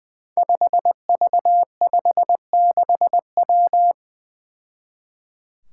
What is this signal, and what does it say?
Morse code, heard as a keyed tone, about 20 wpm, 700 Hz. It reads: 5V56W
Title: Five Challenging Letter-Number Combinations 20wpm